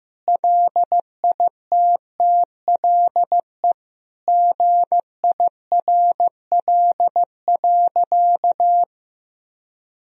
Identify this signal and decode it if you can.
Morse code, heard as a keyed tone, about 15 wpm, 700 Hz. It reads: LITTLE GIRL.
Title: Sentences from Top 1000 Words - Review Part 1 - 15wpm